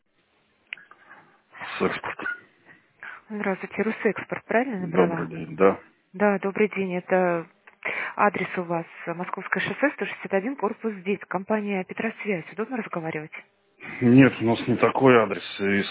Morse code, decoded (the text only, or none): none